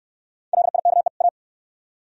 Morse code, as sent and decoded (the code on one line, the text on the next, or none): .-.. .. -.- . / .-
LIKE A